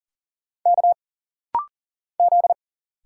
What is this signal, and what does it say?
Morse code, heard as a keyed tone, about 40 wpm, 700 Hz. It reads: K Z
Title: Single Letter-Number Alt - Rapid-Fire 40wpm